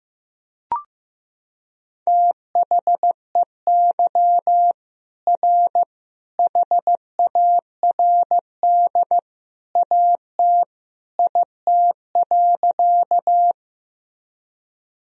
Morse code, heard as a keyed tone, about 15 wpm, 700 Hz. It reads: THEY R HARD AT IT.
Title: Sentences from Top 300 Words and Abbreviations 15wpm